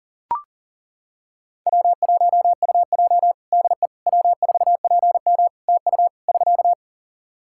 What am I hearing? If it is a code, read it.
Morse code, heard as a keyed tone, about 40 wpm, 700 Hz. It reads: W1UJ DE W4PM TU <SK>
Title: K1USN SST QSO Practice 40wpm